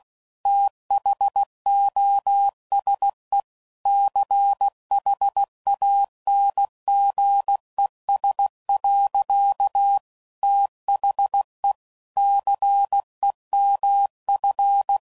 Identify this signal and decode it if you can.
Morse code, heard as a keyed tone, about 16 wpm, 800 Hz. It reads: THOSE CHANGES. THE CEMF
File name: CW Code Groups @15WPM/Technical Subjects